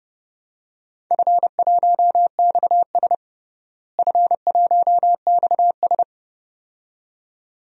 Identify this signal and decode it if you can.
Morse code, heard as a keyed tone, about 30 wpm, 700 Hz. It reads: F1XS F1XS
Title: DXCC-100 Call Signs - 4 Characters - Repeated 2x - 30wpm